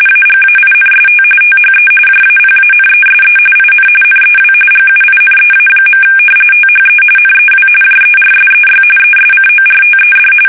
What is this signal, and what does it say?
RTTY, about 45.45 baud. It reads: AGAINST THE JAPANESE IN NEW GUINEA. THE NEW JERSEY ESCORTED
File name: fsk850 / battleshipnj